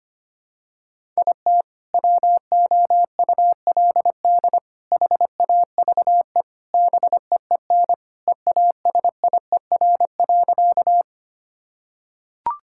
Morse code, heard as a keyed tone, about 25 wpm, 700 Hz. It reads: IT WOULD HAVE BEEN EASIER.
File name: practice / 25wpm / Sentences with 3-Word Phrases - 25wpm